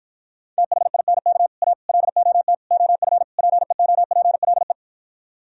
Morse code, heard as a keyed tone, about 52 wpm, 700 Hz. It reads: THINK A LOT OF PEOPLE